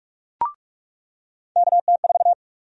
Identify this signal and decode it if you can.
Morse code, heard as a keyed tone, about 46 wpm, 700 Hz. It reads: KT4
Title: USA Call Sign Prefixes - 45wpm